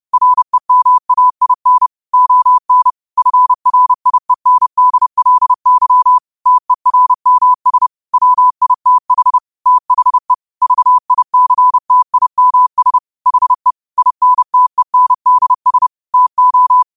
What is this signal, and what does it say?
Morse code, heard as a keyed tone, about 30 wpm, 1000 Hz. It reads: REMAIN ON FRIENDLY TERMS WITH THE VICTIMS HE INTENDS TO